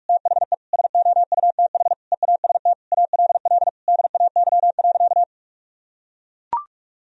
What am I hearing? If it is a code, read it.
Morse code, heard as a keyed tone, about 41 wpm, 700 Hz. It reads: THE SOUTH EAST ALL DAY.
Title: Sentences from Top 1000 Words - Review Part 3 - 45wpm